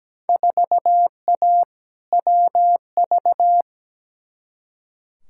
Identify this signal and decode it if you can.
Morse code, heard as a keyed tone, about 17 wpm, 700 Hz. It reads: 4A WV
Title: Field Day Exchanges Encore - 17wpm